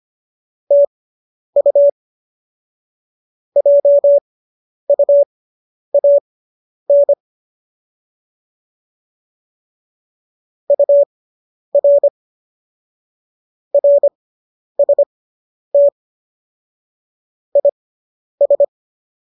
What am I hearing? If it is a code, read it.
Morse code, heard as a keyed tone, about 25 wpm, spaced as for 10 wpm, 570 Hz. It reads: TU JUAN UR RST IS